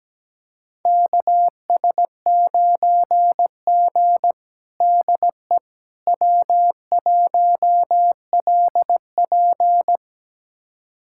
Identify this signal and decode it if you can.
Morse code, heard as a keyed tone, about 17 wpm, 700 Hz. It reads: KS9G DE W1LP
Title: USA Callsign DE Callsign - 4 Characters - 17wpm